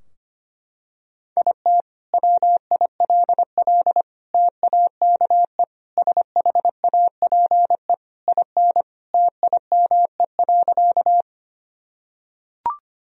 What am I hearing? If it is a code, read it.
Morse code, heard as a keyed tone, about 25 wpm, 700 Hz. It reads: IT WILL TAKE SHAPE IN TIME.
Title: Sentences from Top 1000 Words - Review Part 3 - 25wpm